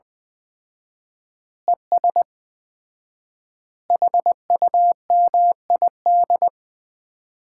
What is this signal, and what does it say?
Morse code, heard as a keyed tone, about 20 wpm, 700 Hz. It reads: ES HUMID
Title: Common QSO Phrases - 4x Word-Spacing - 20wpm